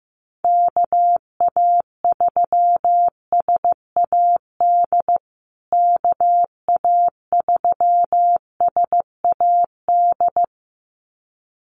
Morse code, heard as a keyed tone, about 15 wpm, 700 Hz. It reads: KA3SAD KA3SAD
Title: USA Call Signs Repeated 2x 15wpm